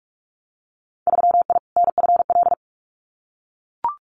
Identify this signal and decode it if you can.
Morse code, heard as a keyed tone, about 44 wpm, 700 Hz. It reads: 3I NFL
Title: Winter Field Day Exchanges 45wpm